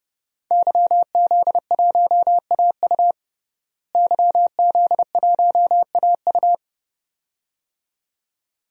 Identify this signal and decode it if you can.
Morse code, heard as a keyed tone, about 32 wpm, 700 Hz. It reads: YZ1AU YZ1AU
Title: DXCC-100 Call Signs - 5 Characters - Repeated 2x - 30wpm